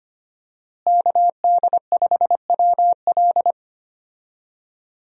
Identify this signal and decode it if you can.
Morse code, heard as a keyed tone, about 25 wpm, 700 Hz. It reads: KD5WL
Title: USA Call Signs - 5 Characters - Repeated 2x - 25wpm